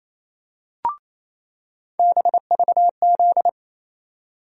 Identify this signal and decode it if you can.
Morse code, heard as a keyed tone, about 28 wpm, 700 Hz. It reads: BVZ
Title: USA Call Sign Postfixes - 3 Letters 28wpm